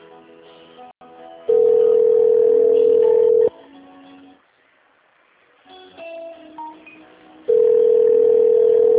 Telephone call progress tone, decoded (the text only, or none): ringback tone